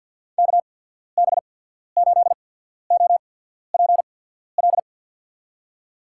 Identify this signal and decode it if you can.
Morse code, heard as a keyed tone, about 53 wpm, spaced as for 14 wpm, 700 Hz. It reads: KB8OPL
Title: USA Call Signs - 6 Characters - 8x Character Spacing - 50wpm